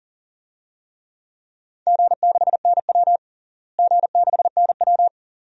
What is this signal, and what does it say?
Morse code, heard as a keyed tone, about 37 wpm, 700 Hz. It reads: G6NW G6NW
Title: DXCC-100 Call Signs - 4 Characters - Repeated 2x - 40wpm